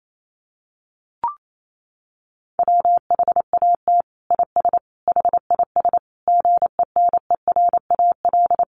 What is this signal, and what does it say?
Morse code, heard as a keyed tone, about 28 wpm, 700 Hz. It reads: WHAT IS HIS GENERAL
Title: Sentences from Top 1000 Words 28wpm